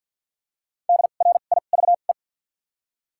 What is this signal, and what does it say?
Morse code, heard as a keyed tone, about 53 wpm, spaced as for 37 wpm, 700 Hz. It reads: DRIVE
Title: 2x Character Spacing - Top 500 Words - 50wpm